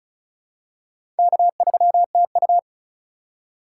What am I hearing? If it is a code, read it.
Morse code, heard as a keyed tone, about 36 wpm, 700 Hz. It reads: K3TU
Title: USA Call Signs - 3x Character Spacing - 35wpm